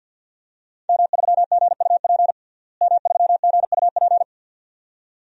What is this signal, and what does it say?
Morse code, heard as a keyed tone, about 52 wpm, 700 Hz. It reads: M3GUP M3GUP
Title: DXCC-100 Call Signs - 5 Characters - Repeated 2x - 50wpm